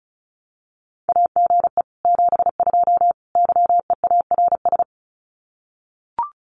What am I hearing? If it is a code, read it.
Morse code, heard as a keyed tone, about 34 wpm, 700 Hz. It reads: AGE 72 YEARS
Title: Common QSO Phrases Encore 35wpm